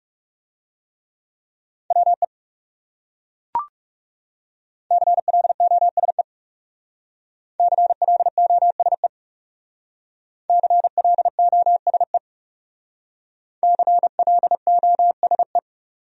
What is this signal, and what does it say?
Morse code, heard as a keyed tone, about 40 wpm, 700 Hz. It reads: WE CLOSE CLOSE CLOSE CLOSE